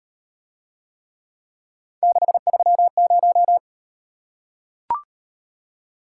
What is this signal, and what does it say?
Morse code, heard as a keyed tone, about 38 wpm, 700 Hz. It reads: 630